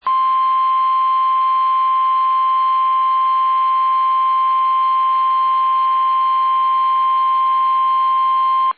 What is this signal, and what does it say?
Emergency Alert System (SAME) tones (the 1050 Hz attention tone). It is heard from 0.5 to 8.2 s.
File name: eas / KUSO / KUSO